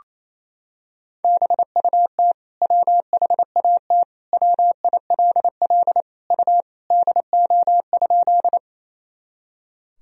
Morse code, heard as a keyed tone, about 28 wpm, 700 Hz. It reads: BUT WHAT WILL U DO?
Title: Sentences from Top 100 Words and Abbreviations 28wpm